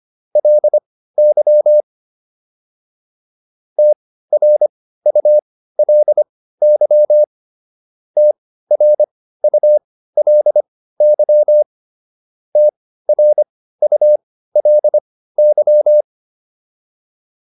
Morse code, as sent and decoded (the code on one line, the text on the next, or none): .-.. -.-- / - .-. ..- .-.. -.-- / - .-. ..- .-.. -.-- / - .-. ..- .-.. -.--
LY TRULY TRULY TRULY